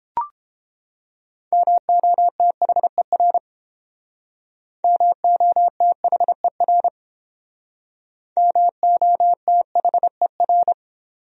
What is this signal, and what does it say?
Morse code, heard as a keyed tone, about 30 wpm, 700 Hz. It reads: MOTHER MOTHER MOTHER